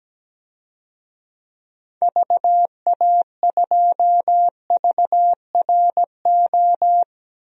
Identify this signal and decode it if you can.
Morse code, heard as a keyed tone, about 17 wpm, 700 Hz. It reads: VA2VRO
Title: DXCC-100 Call Signs - 6 Characters - Repeated 2x - 17wpm